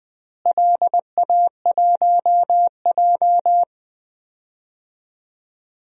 Morse code, heard as a keyed tone, about 21 wpm, 700 Hz. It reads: LA1J
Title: DXCC-100 Call Signs - 4 Characters - Repeated 3x - 20wpm